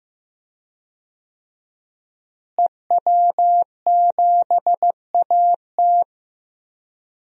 Morse code, heard as a keyed tone, about 15 wpm, 700 Hz. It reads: EW7AT
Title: DXCC-100 Call Signs - 5 Characters - Repeated 3x - 15wpm